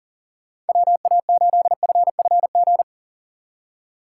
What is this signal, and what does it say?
Morse code, heard as a keyed tone, about 46 wpm, 700 Hz. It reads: WA8FFG
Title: Speed Racing - USA Call Signs 40wpm